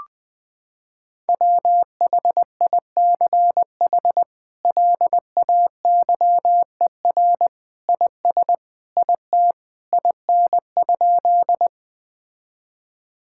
Morse code, heard as a keyed tone, about 20 wpm, 700 Hz. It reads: WHICH LAYER IS IT IN?
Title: Sentences from Top 1600 Words 20wpm